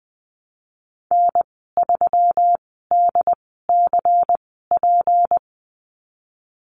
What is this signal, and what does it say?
Morse code, heard as a keyed tone, about 19 wpm, spaced as for 14 wpm, 700 Hz. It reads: N3DCP
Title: 245 Lesson 39g - Letter z - Call Signs 20wpm